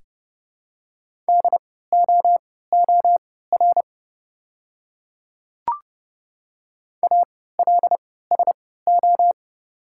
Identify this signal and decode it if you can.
Morse code, heard as a keyed tone, about 30 wpm, spaced as for 17 wpm, 690 Hz. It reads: DOOR ALSO
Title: 3x Character Spacing - Top 300 Words - 30wpm